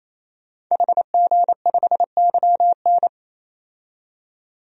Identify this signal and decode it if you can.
Morse code, heard as a keyed tone, about 27 wpm, 700 Hz. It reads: HG5YN
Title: DXCC-100 Call Signs - 5 Characters - 28wpm